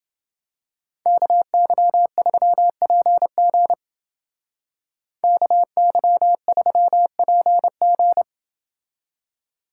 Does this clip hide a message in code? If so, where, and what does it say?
Yes: Morse code from 1.1 to 8.2 s, heard as a keyed tone, about 29 wpm, 700 Hz. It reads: KY3PG KY3PG